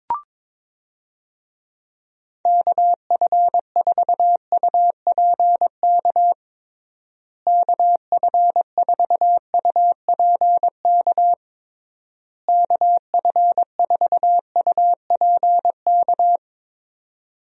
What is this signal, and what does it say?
Morse code, heard as a keyed tone, about 22 wpm, 700 Hz. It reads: KF4UPK KF4UPK KF4UPK
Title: USA Call Signs - 6 Characters - Repeated 3x - 22wpm